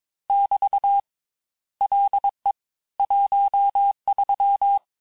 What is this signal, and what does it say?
Morse code, heard as a keyed tone, about 22 wpm, 800 Hz. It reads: <BT> LE 13
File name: texte clair vitesse 16 puis 21 mpm